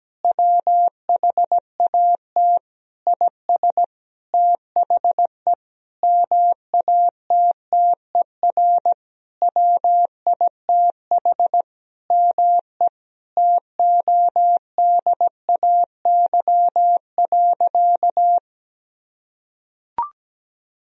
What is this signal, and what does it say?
Morse code, heard as a keyed tone, about 17 wpm, 700 Hz. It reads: WHAT IS THE MATTER WITH ME TODAY.
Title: Sentences from Top 1000 Words - Review Part 4 - 17wpm